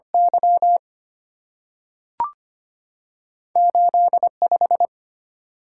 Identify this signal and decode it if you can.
Morse code, heard as a keyed tone, about 25 wpm, 700 Hz. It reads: Y 85
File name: Two Letter-Number Combinations - 25wpm